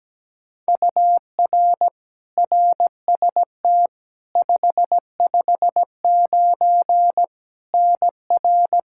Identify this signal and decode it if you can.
Morse code, heard as a keyed tone, about 17 wpm, 700 Hz. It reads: UR RST 559 NR
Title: K1USN SST QSO Practice 17wpm